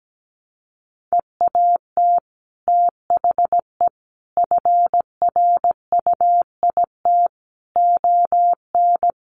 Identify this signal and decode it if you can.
Morse code, heard as a keyed tone, about 17 wpm, 700 Hz. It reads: EAT THE FRUIT ON
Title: Sentences from Top 1300 Words 17wpm